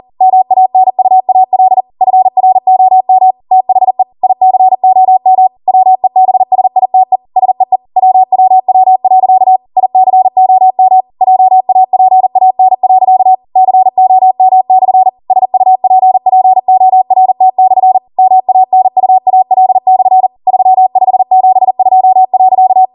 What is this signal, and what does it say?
Morse code, heard as a keyed tone, about 40 wpm, 750 Hz. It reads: MANUAL FROM THE ICOM WEBSITE SEE WWW.ICOMJAPAN.COM/SUPPORT/MANUAL/3572.
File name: Morse / 40WPM